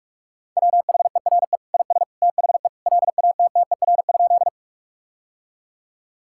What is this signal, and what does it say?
Morse code, heard as a keyed tone, about 41 wpm, 700 Hz. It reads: WHERE IS THE LATTER?
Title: Sentences from Top 1600 Words 45wpm